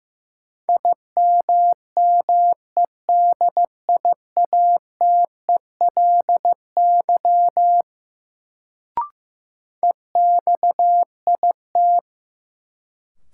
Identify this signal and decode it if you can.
Morse code, heard as a keyed tone, about 15 wpm, 700 Hz. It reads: IMMEDIATELY EXIT
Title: Sentences from Top 1100 Words - New Word Drill - 15wpm